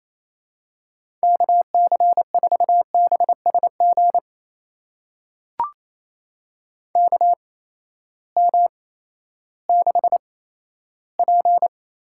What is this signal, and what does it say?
Morse code, heard as a keyed tone, about 28 wpm, spaced as for 14 wpm, 700 Hz. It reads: KC4BSG KM6P
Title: USA Call Signs - 6 Characters - 8x Character Spacing - 28wpm